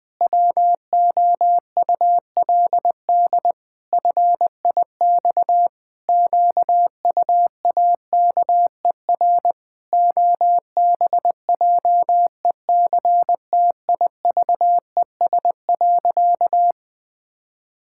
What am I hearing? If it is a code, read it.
Morse code, heard as a keyed tone, about 20 wpm, 700 Hz. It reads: WOULD FIX QUAKER OBJECTIVES.
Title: Pangram Challenge v2 - 20wpm